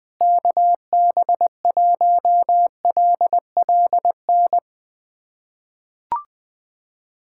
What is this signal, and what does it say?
Morse code, heard as a keyed tone, about 20 wpm, 700 Hz. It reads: KB1LLN
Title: USA Callsign DE Callsign - 6 Characters - 20wpm